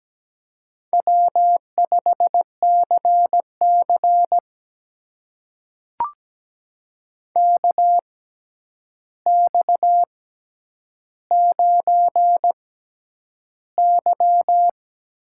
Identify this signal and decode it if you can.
Morse code, heard as a keyed tone, about 17 wpm, spaced as for 9 wpm, 700 Hz. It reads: W5CC KX9Y
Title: USA Call Signs - 4 Characters - 6x Character Spacing - 17wpm